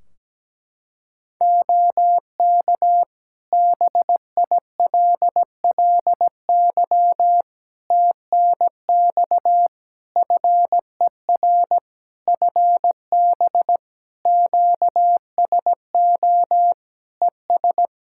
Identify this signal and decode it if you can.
Morse code, heard as a keyed tone, about 17 wpm, 700 Hz. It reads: OK BILLY TNX FER FB QSO ES
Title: Practice QSO - Exchange 4 - Ending - 17wpm